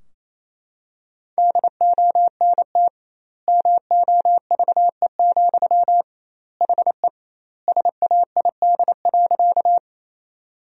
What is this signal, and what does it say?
Morse code, heard as a keyed tone, about 28 wpm, 700 Hz. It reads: DONT MOVE, HE SAID.
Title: 2x Word-Spacing - Sentences from Top 300 Words - 28wpm